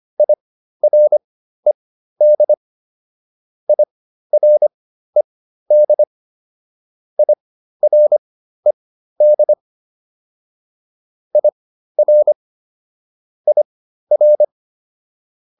Morse code, heard as a keyed tone, about 25 wpm, spaced as for 13 wpm, 600 Hz. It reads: IRED IRED IRED IR IR